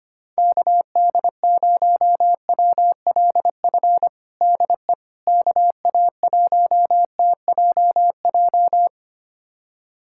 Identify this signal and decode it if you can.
Morse code, heard as a keyed tone, about 25 wpm, 700 Hz. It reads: KD0WLF DE KA1TJJ
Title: USA Callsign DE Callsign - 6 Characters - 25wpm